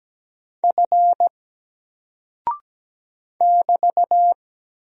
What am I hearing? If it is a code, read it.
Morse code, heard as a keyed tone, about 17 wpm, 700 Hz. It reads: F <BT>
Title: Single Character (Letter:Number:Punctuation:Prosign) Alt 17wpm